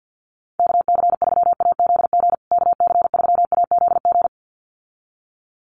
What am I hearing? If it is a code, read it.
Morse code, heard as a keyed tone, about 50 wpm, 700 Hz. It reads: KC3AZG KC3AZG
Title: USA Call Signs Repeated 2x 50wpm